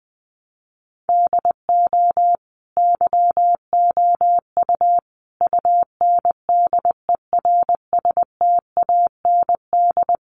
Morse code, heard as a keyed tone, about 20 wpm, 700 Hz. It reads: DO YOU UNDERSTAND
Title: Sentences from Top 600 Words 20wpm